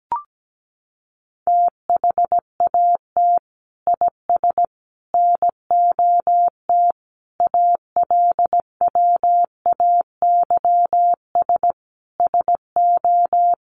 Morse code, heard as a keyed tone, about 17 wpm, 700 Hz. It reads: THAT IS NOT ALWAYS SO
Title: Sentences from Top 700 Words 17wpm